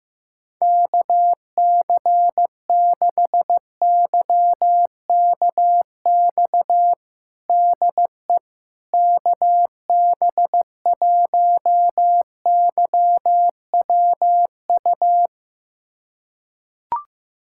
Morse code, heard as a keyed tone, about 15 wpm, 700 Hz. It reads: KC6YKX DE KB1YWU
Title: USA Callsign DE Callsign - 6 Characters - 15wpm